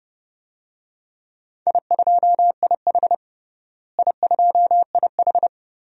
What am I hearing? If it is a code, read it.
Morse code, heard as a keyed tone, about 30 wpm, 700 Hz. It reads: I2IH I2IH
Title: DXCC-100 Call Signs - 4 Characters - Repeated 2x - 30wpm